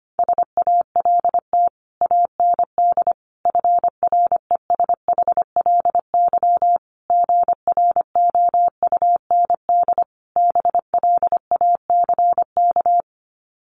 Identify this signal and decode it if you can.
Morse code, heard as a keyed tone, about 25 wpm, 700 Hz. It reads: SALT AND FRESHLY GROUND BLACK
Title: Sets of 5 Words Encore 25wpm